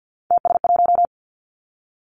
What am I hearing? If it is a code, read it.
Morse code, heard as a keyed tone, about 50 wpm, 700 Hz. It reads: TS1
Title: Three Challenging Letter-Number Combinations 50wpm